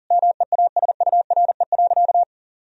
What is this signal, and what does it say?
Morse code, heard as a keyed tone, about 37 wpm, 700 Hz. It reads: MEASURE.
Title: Sentences from Top 1000 Words - Review Part 2 - 40wpm